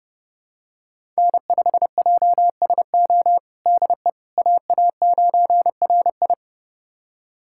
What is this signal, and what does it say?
Morse code, heard as a keyed tone, about 32 wpm, 700 Hz. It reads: N5JSO DE AA9RI
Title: USA Callsign DE Callsign - 5 Characters - 30wpm